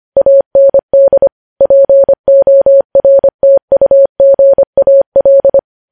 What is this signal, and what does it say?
Morse code, heard as a keyed tone, about 25 wpm, 550 Hz. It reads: AND PORTUGAL
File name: podcast / News headlines-1x-25x25